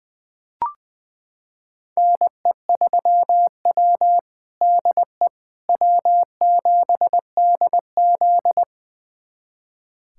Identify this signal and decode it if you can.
Morse code, heard as a keyed tone, about 20 wpm, 700 Hz. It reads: NE3W DE W7DZ
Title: USA Callsign DE Callsign - 4 Characters - 20wpm